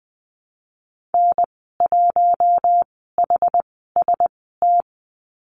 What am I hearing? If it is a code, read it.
Morse code, heard as a keyed tone, about 20 wpm, spaced as for 15 wpm, 700 Hz. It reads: N1HST
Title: 076 Lesson 13g - Letter L - Call Signs 20wpm